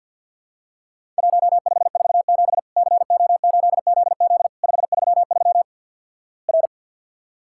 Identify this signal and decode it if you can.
Morse code, heard as a keyed tone, about 50 wpm, 700 Hz. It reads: 1547 CO8ZZ 533 R
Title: Worked All Europe DX Contest - QTC Exchange with Repeats - 50wpm